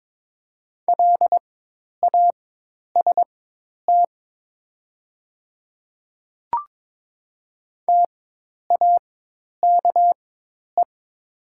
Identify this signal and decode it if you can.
Morse code, heard as a keyed tone, about 22 wpm, spaced as for 10 wpm, 700 Hz. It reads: LAST TAKE
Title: 4x Character Spacing - Top 300 Words - 22wpm